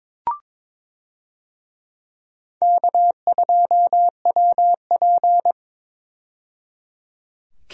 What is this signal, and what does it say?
Morse code, heard as a keyed tone, about 22 wpm, 700 Hz. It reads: K2WP
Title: USA Call Signs - 4 Characters - 22wpm